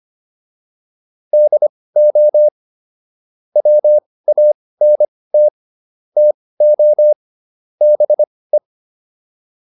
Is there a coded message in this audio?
Yes: Morse code from 1.3 to 8.6 s, heard as a keyed tone, about 25 wpm, spaced as for 18 wpm, 600 Hz. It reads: DO WANT TO BE